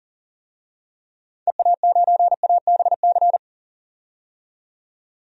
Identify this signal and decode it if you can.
Morse code, heard as a keyed tone, about 40 wpm, 700 Hz. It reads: EA9ABC
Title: DXCC-100 Call Signs - 6 Characters - 40wpm